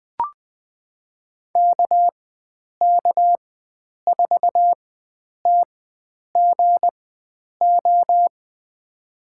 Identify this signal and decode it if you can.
Morse code, heard as a keyed tone, about 20 wpm, spaced as for 9 wpm, 700 Hz. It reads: KK4TGO
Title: USA Call Signs - 6 Characters - 4x Character Spacing - 20wpm